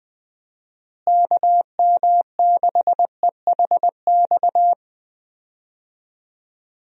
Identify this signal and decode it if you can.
Morse code, heard as a keyed tone, about 20 wpm, 700 Hz. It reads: KM6EHX
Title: USA Call Signs - 6 Characters - Repeated 2x - 20wpm